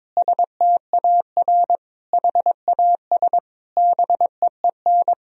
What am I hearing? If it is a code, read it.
Morse code, heard as a keyed tone, about 22 wpm, 700 Hz. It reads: STAR HAS BEEN
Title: Sentences from Top 1000 Words - Review Part 2 - 22wpm